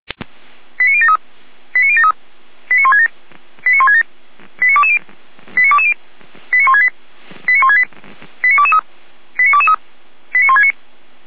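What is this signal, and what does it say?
Five-tone selective call (ZVEI), heard as quick runs of five tones, tones of about 70 ms. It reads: 89073 89073 89167 89167 89220 89220 89168 89168 89223 89223 89169
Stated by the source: ZVEI five-tone selective call received with an SDR, archived by the Feuerwehr Orlamünde fire brigade.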